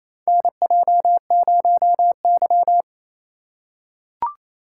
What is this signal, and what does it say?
Morse code, heard as a keyed tone, about 28 wpm, 690 Hz. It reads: NJ0Y
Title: USA Call Signs - 4 Characters - 2x Character Spacing - 28wpm